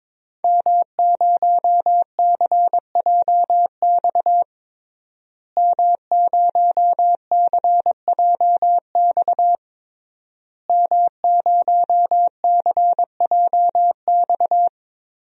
Morse code, heard as a keyed tone, about 22 wpm, 700 Hz. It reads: M0CJX M0CJX M0CJX
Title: DXCC-100 Call Signs - 5 Characters - Repeated 3x - 22wpm